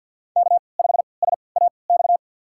Morse code, heard as a keyed tone, about 47 wpm, spaced as for 28 wpm, 700 Hz. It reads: K5SAX
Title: USA Call Signs - 5 Characters - 3x Character Spacing - 50wpm